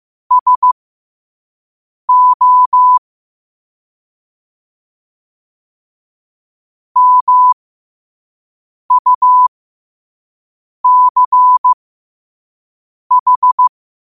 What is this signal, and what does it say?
Morse code, heard as a keyed tone, about 15 wpm, spaced as for 5 wpm, 1000 Hz. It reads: SO MUCH
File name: qotdfiles / 5wpm